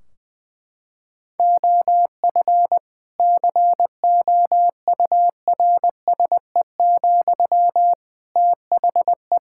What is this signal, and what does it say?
Morse code, heard as a keyed tone, about 20 wpm, 700 Hz. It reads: OF COURSE, THE
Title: Sentences from Top 1600 Words 20wpm